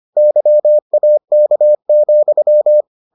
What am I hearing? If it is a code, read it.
Morse code, heard as a keyed tone, about 25 wpm, 600 Hz. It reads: YAK,